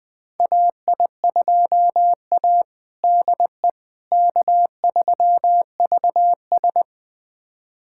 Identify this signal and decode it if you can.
Morse code, heard as a keyed tone, about 20 wpm, 700 Hz. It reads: AI2A DE K3VS
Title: USA Callsign DE Callsign - 4 Characters - 20wpm